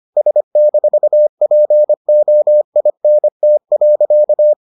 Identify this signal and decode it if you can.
Morse code, heard as a keyed tone, about 25 wpm, 600 Hz. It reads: S-POINT.